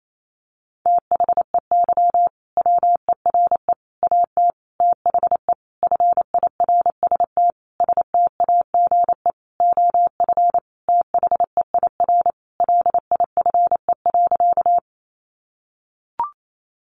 Morse code, heard as a keyed tone, about 28 wpm, 700 Hz. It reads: THEY WERE AT THE FIRST STAGE OF THEIR LIFE.
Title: Sentences from Top 1200 Words 28wpm